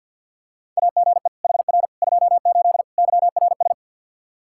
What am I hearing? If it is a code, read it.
Morse code, heard as a keyed tone, about 46 wpm, 700 Hz. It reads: AGE HR 28 YRS
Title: Practice QSO - Exchange 3 - More Info - 50wpm